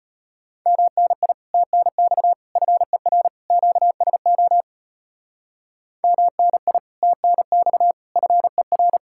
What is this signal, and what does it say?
Morse code, heard as a keyed tone, about 38 wpm, 700 Hz. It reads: MNI TNX FER QSO MNI TNX FER